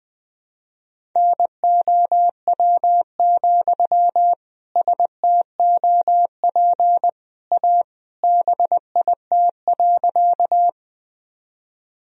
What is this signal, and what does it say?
Morse code, heard as a keyed tone, about 20 wpm, 700 Hz. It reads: NOW, STOP A BIT.
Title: Sentences from Top 1000 Words - Review Part 1 - 20wpm